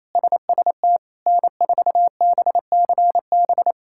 Morse code, heard as a keyed tone, about 28 wpm, 700 Hz. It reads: SST N4BCB
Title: K1USN SST QSO Practice - 28wpm